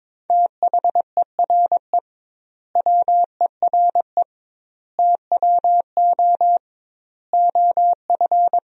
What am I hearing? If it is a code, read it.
Morse code, heard as a keyed tone, about 22 wpm, 700 Hz. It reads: THERE WERE TWO OF
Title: 2x Word-Spacing - Sentences from Top 100 Words - 22wpm